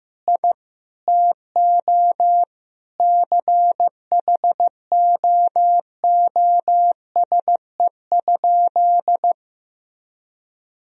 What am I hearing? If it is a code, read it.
Morse code, heard as a keyed tone, about 16 wpm, 700 Hz. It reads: I TO CHOOSE?
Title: Sentences from Top 1000 Words - Review Part 4 - 15wpm